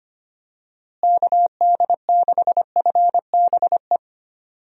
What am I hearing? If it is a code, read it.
Morse code, heard as a keyed tone, about 25 wpm, 700 Hz. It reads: KD6FBE
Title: USA Call Signs - 6 Characters - Repeated 2x - 25wpm